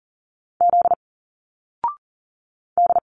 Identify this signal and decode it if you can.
Morse code, heard as a keyed tone, about 40 wpm, 700 Hz. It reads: Z D
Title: Single Letters 40wpm